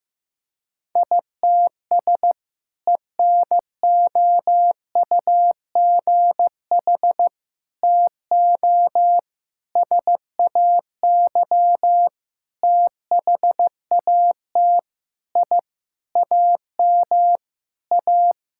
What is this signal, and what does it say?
Morse code, heard as a keyed tone, about 16 wpm, 700 Hz. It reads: ITS ENOUGH TO SAY THAT I AM A
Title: Sentences from Top 1000 Words - Review Part 6 - 15wpm